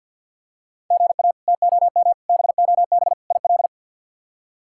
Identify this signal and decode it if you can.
Morse code, heard as a keyed tone, about 52 wpm, 700 Hz. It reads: GA TOM BOB IL
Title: K1USN SST QSO Practice - 50wpm